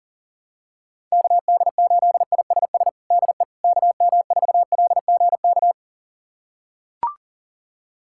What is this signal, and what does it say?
Morse code, heard as a keyed tone, about 40 wpm, 700 Hz. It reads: KD8ISS DE KM4LGK
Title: USA Callsign DE Callsign - 6 Characters - 40wpm